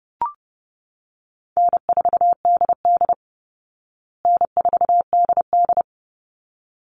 Morse code, heard as a keyed tone, about 28 wpm, 700 Hz. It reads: N4DD N4DD
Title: Call Signs Repeated 3x - 30wpm